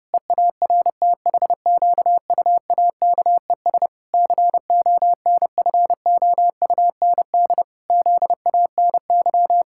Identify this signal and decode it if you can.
Morse code, heard as a keyed tone, about 30 wpm, 700 Hz. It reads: EARTHQUAKES CONFOUND ZANY
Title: Pangram Challenge v2 - 30wpm